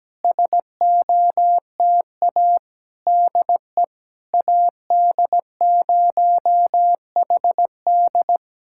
Morse code, heard as a keyed tone, about 17 wpm, 700 Hz. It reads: SOTA DE AD0HD
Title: SOTA QSO Practice 17wpm